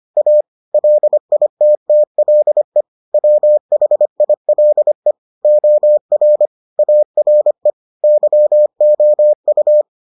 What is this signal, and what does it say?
Morse code, heard as a keyed tone, about 25 wpm, 600 Hz. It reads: A LITTLE WHILE OR ARE YOU